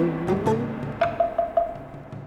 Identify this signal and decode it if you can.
There is no code.